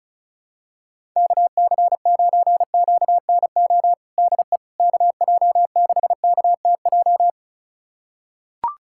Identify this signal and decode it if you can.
Morse code, heard as a keyed tone, about 36 wpm, 700 Hz. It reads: KC9QNO DE KJ6KTJ
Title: USA Callsign DE Callsign 35wpm